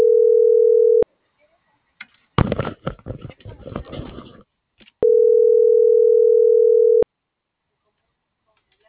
Ringback tone (North America), 440 + 480 Hz.